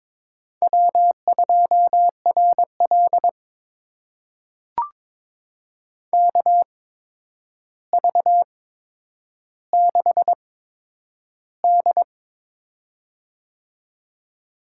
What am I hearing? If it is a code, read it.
Morse code, heard as a keyed tone, about 22 wpm, spaced as for 9 wpm, 700 Hz. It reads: W2RL KV6D
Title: USA Call Signs - 4 Characters - 8x Character Spacing - 22wpm